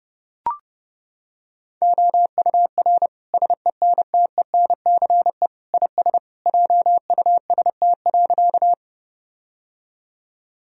Morse code, heard as a keyed tone, about 32 wpm, 700 Hz. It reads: OUR SENTENCE IS JUST.
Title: Sentences from Top 1000 Words - Review Part 1 - 30wpm